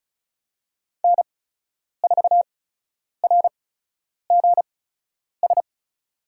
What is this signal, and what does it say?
Morse code, heard as a keyed tone, about 36 wpm, spaced as for 10 wpm, 700 Hz. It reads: N4RGS
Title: USA Call Signs - 5 Characters - 8x Character Spacing - 35wpm